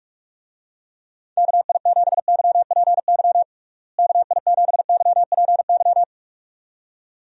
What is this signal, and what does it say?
Morse code, heard as a keyed tone, about 46 wpm, 700 Hz. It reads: KI7YPY KI7YPY
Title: USA Call Signs - 6 Characters - Repeated 2x - 45wpm